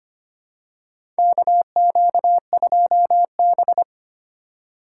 Morse code, heard as a keyed tone, about 25 wpm, 700 Hz. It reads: KQ2B
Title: USA Call Signs - 4 Characters - 2x Character Spacing - 25wpm